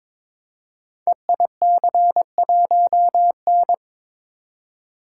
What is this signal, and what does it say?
Morse code, heard as a keyed tone, about 22 wpm, 700 Hz. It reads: EIC1N
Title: Code Groups - 5 Letter:Numbers - 22wpm